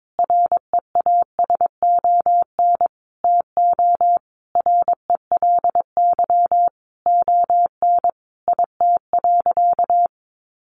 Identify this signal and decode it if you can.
Morse code, heard as a keyed tone, about 22 wpm, 700 Hz. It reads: REASON TO RELY ON IT.
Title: Sentences from Top 1200 Words 22wpm